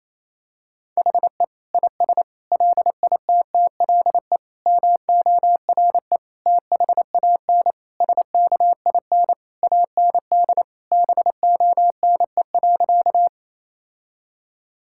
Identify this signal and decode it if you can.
Morse code, heard as a keyed tone, about 28 wpm, 700 Hz. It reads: HE IS LITTLE MORE THAN SKIN AND BONE.
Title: Sentences from Top 1600 Words 28wpm